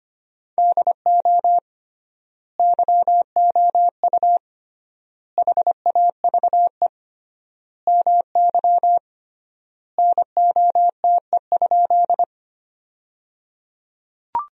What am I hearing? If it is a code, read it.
Morse code, heard as a keyed tone, about 25 wpm, 700 Hz. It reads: DO YOU HAVE MY NOTE?
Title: 3x Word-Spacing - Sentences from Top 500 Words - 25wpm